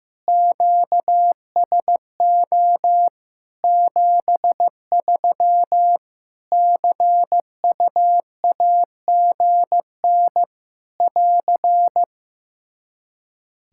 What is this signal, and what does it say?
Morse code, heard as a keyed tone, about 15 wpm, 700 Hz. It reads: QSO 73 CUAGN <AR>
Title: Practice QSO - Exchange 4 - Ending - 15wpm